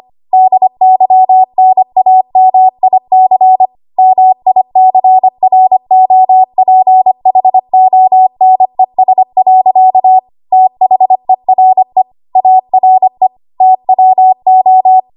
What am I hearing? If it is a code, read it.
Morse code, heard as a keyed tone, about 25 wpm, 750 Hz. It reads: DYNAMIC MICROPHONES. THERE ARE TWO
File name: Archive / 25 WPM / 25WPM